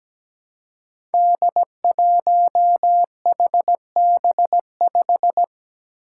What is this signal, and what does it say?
Morse code, heard as a keyed tone, about 17 wpm, 700 Hz. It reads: D1HB5
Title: Five Challenging Letter-Number Combinations 17wpm